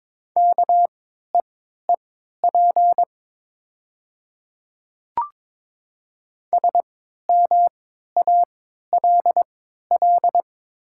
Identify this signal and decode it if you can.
Morse code, heard as a keyed tone, about 22 wpm, spaced as for 12 wpm, 700 Hz. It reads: KEEP SMALL
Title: 3x Character Spacing - Top 300 Words - 22wpm